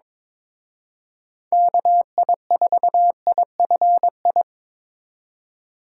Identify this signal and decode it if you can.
Morse code, heard as a keyed tone, about 25 wpm, 700 Hz. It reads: KI4IFI
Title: Speed Racing - USA 2x3 Call Signs 22wpm